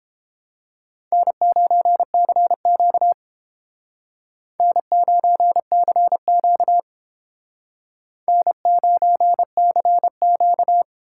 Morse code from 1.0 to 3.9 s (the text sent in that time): N9CQ